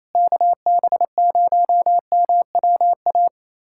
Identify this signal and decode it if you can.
Morse code, heard as a keyed tone, about 28 wpm, 700 Hz. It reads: KB0MWA
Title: USA Callsign DE Callsign - 6 Characters - 28wpm